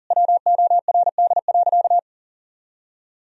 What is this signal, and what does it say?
Morse code, heard as a keyed tone, about 40 wpm, 700 Hz. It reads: WORD.
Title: Sentences from Top 1000 Words - Review Part 3 - 40wpm